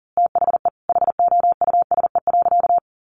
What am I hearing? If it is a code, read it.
Morse code, heard as a keyed tone, about 40 wpm, 690 Hz. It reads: THE HOUSE.
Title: Sentences from Top 800 Words 40wpm